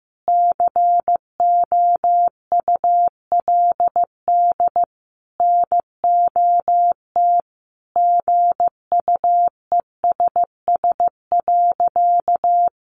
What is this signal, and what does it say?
Morse code, heard as a keyed tone, about 15 wpm, 700 Hz. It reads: COULD NOT GUESS.
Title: Sentences from Top 1500 Words 15wpm